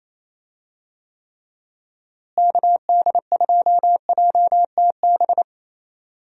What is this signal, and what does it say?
Morse code, heard as a keyed tone, about 27 wpm, 700 Hz. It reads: KD2JTB
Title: USA Call Signs - 6 Characters - Repeated 3x - 28wpm